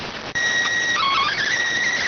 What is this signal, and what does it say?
SSTV (slow-scan television), PD 290.